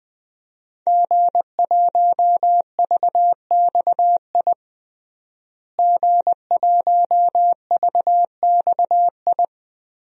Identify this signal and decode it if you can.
Morse code, heard as a keyed tone, about 20 wpm, 700 Hz. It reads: G1VXI G1VXI
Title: DXCC-100 Call Signs - 5 Characters - Repeated 3x - 20wpm